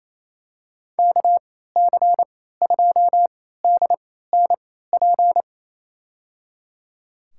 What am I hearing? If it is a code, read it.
Morse code, heard as a keyed tone, about 28 wpm, spaced as for 16 wpm, 700 Hz. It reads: KC2DNP
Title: USA Call Signs - 6 Characters - 3x Character Spacing - 28wpm